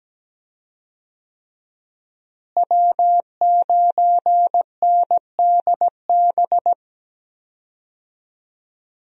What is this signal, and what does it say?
Morse code, heard as a keyed tone, about 17 wpm, 700 Hz. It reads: W9NDB
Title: USA Call Signs - 5 Characters - Repeated 3x - 17wpm